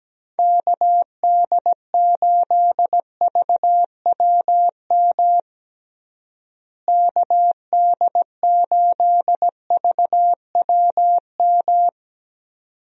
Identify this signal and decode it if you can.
Morse code, heard as a keyed tone, about 17 wpm, 700 Hz. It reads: KD8VWM KD8VWM
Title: USA Call Signs - 6 Characters - Repeated 2x - 17wpm